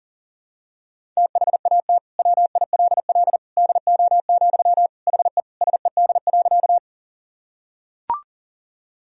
Morse code, heard as a keyed tone, about 40 wpm, 700 Hz. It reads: THAT WILL DO, HE SED.
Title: Sentences from Top 100 Words and Abbreviations 40wpm